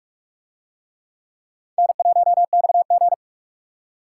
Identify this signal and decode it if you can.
Morse code, heard as a keyed tone, about 49 wpm, 700 Hz. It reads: N1XG